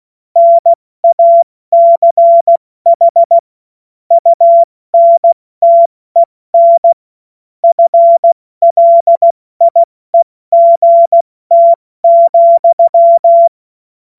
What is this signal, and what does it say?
Morse code, heard as a keyed tone, about 16 wpm, spaced as for 14 wpm, 680 Hz. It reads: NACH UNTEN FLIEGT,
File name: Morsekurs / Witze / Witz 02. Dezember 2025 70 F80